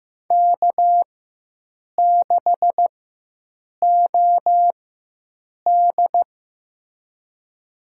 Morse code, heard as a keyed tone, about 15 wpm, spaced as for 7 wpm, 700 Hz. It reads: K6OD
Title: USA Call Signs - 4 Characters - 4x Character Spacing - 15wpm